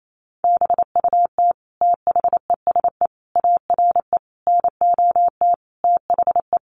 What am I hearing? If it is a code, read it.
Morse code, heard as a keyed tone, about 28 wpm, 700 Hz. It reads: BUT THESE ARE NOT THE
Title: Sentences from Top 1300 Words 28wpm